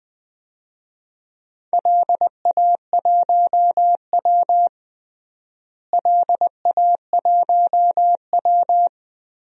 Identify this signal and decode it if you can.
Morse code, heard as a keyed tone, about 21 wpm, 700 Hz. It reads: LA1W LA1W
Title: DXCC-100 Call Signs - 4 Characters - Repeated 3x - 20wpm